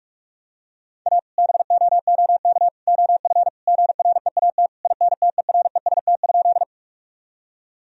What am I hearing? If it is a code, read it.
Morse code, heard as a keyed tone, about 41 wpm, 700 Hz. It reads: A BOOK OF GREAT INTEREST?
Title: Sentences from Top 1000 Words - Review Part 2 - 45wpm